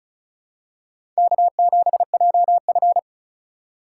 Morse code, heard as a keyed tone, about 34 wpm, 700 Hz. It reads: K7JF
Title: USA Call Signs - 2x Character Spacing - 35wpm